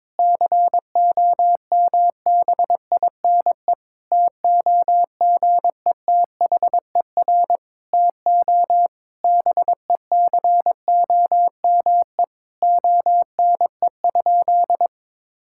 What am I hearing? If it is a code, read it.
Morse code, heard as a keyed tone, about 22 wpm, 700 Hz. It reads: COMBINE TOGETHER TO BECOME ONE?
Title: Sentences from Top 1600 Words 22wpm